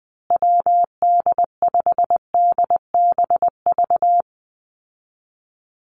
Morse code, heard as a keyed tone, about 20 wpm, 700 Hz. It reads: WD5DBV
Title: Call Signs Repeated 2x - 20wpm